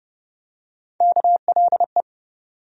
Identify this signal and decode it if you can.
Morse code, heard as a keyed tone, about 30 wpm, 700 Hz. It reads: KLE
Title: USA Call Sign Postfixes - 3 Letters 30wpm